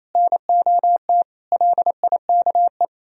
Morse code, heard as a keyed tone, about 33 wpm, 700 Hz. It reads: NOT LIKE
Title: Speed-Racing - Sets of 4 Words 28wpm